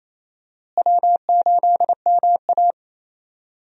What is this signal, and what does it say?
Morse code, heard as a keyed tone, about 28 wpm, 700 Hz. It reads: W8MA
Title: USA Call Signs - 4 Characters - Repeated 2x - 28wpm